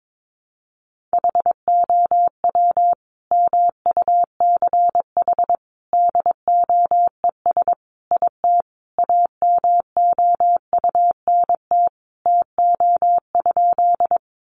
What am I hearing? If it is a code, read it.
Morse code, heard as a keyed tone, about 22 wpm, 700 Hz. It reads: HOW MUCH DOES IT AMOUNT TO?
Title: Sentences from Top 700 Words 22wpm